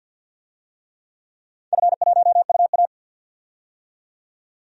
Morse code, heard as a keyed tone, about 52 wpm, 700 Hz. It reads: F1UA
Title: DXCC-100 Call Signs - 4 Characters - 50wpm